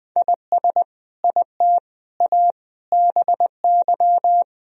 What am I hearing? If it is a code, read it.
Morse code, heard as a keyed tone, about 20 wpm, 700 Hz. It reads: IS IT A BY
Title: Sentences from Top 1000 Words - Review Part 2 - 20wpm